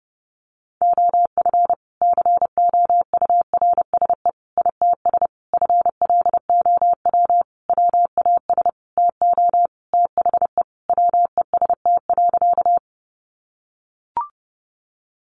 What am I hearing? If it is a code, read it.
Morse code, heard as a keyed tone, about 28 wpm, 700 Hz. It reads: OF COURSE ITS FLOW WAS TO THE WEST.
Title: Sentences from Top 1200 Words 30wpm